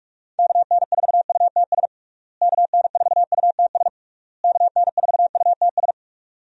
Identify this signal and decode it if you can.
Morse code, heard as a keyed tone, about 41 wpm, 700 Hz. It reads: KN4UTS KN4UTS KN4UTS
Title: USA Call Signs - 6 Characters - Repeated 3x - 45wpm